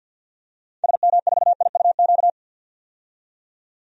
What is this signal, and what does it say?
Morse code, heard as a keyed tone, about 52 wpm, 700 Hz. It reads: SM4IUQ
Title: DXCC-100 Call Signs - 6 Characters - 50wpm